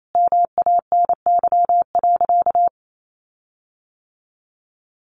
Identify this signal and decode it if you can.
Morse code, heard as a keyed tone, about 28 wpm, 700 Hz. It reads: MANY.
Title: Sentences from Top 1300 Words 28wpm